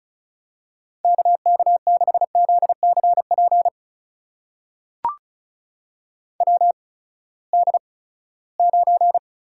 Morse code, heard as a keyed tone, about 36 wpm, spaced as for 20 wpm, 700 Hz. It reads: KK6ZCP WD9
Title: USA Call Signs - 8x Character Spacing - 35wpm